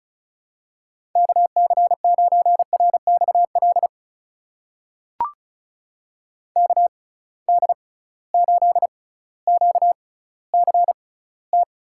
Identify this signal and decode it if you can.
Morse code, heard as a keyed tone, about 36 wpm, spaced as for 18 wpm, 700 Hz. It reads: KC9RXL KD8QCT
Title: USA Call Signs - 6 Characters - 6x Character Spacing - 35wpm